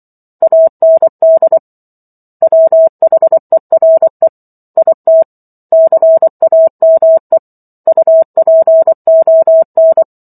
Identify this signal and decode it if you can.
Morse code, heard as a keyed tone, about 24 wpm, 650 Hz. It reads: AND WHERE IT CAME UPON